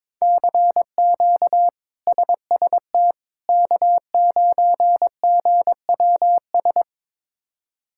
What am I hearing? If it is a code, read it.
Morse code, heard as a keyed tone, about 21 wpm, 700 Hz. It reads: CQ SST K9GWS
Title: K1USN SST QSO Practice - 22wpm